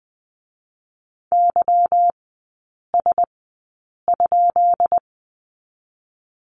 Morse code, heard as a keyed tone, about 20 wpm, 690 Hz. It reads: Y S ?
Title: 145 Lesson 24e - Letter y - Three Characters 20wpm